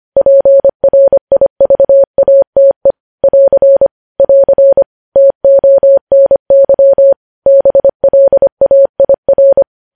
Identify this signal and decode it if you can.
Morse code, heard as a keyed tone, about 25 wpm, 550 Hz. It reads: PRIVATE <AR> <AR> TONY BLAIR
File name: podcast / News headlines-1x-25x25